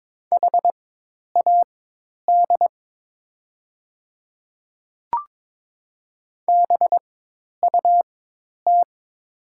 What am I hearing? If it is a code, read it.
Morse code, heard as a keyed tone, about 22 wpm, spaced as for 10 wpm, 700 Hz. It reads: HAD BUT
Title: 4x Character Spacing - Top 300 Words - 22wpm